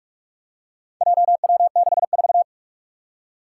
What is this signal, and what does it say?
Morse code, heard as a keyed tone, about 46 wpm, 700 Hz. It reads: JW64
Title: Four Challenging Letter-Number Combinations 45wpm